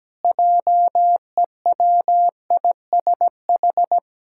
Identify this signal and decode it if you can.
Morse code, heard as a keyed tone, about 17 wpm, 700 Hz. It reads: JEWISH
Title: Pangram Challenge v2 - 17wpm